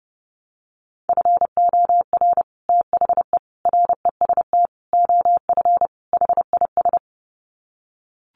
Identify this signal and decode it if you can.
Morse code, heard as a keyed tone, about 28 wpm, 700 Hz. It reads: FOR THE REST OF HIS
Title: Sets of 5 Words 30wpm